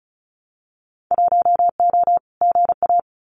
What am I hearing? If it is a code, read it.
Morse code, heard as a keyed tone, about 36 wpm, 700 Hz. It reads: 1O GA
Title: Winter Field Day Exchanges 35wpm